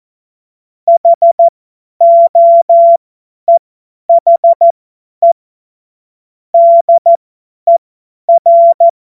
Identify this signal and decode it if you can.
Morse code, heard as a keyed tone, about 14 wpm, spaced as for 10 wpm, 680 Hz. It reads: HOEHE DER
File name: Morsekurs / Witze / Witz 21. Juni 2022 50 F70